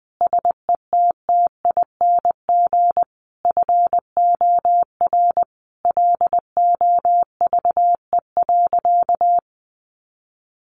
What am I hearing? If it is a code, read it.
Morse code, heard as a keyed tone, about 20 wpm, 700 Hz. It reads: SETTING FOR LOVE.
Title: Sentences from Top 1300 Words 20wpm